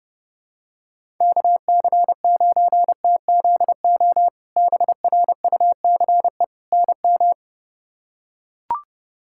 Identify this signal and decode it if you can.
Morse code, heard as a keyed tone, about 30 wpm, 700 Hz. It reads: KC9TZO BRUCE NM
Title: K1USN SST QSO Practice - 30wpm